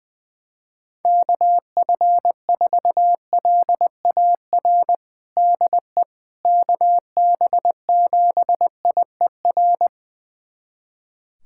Morse code, heard as a keyed tone, about 20 wpm, 700 Hz. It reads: KF4LAR DE KB7IER
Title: USA Callsign DE Callsign - 6 Characters - 20wpm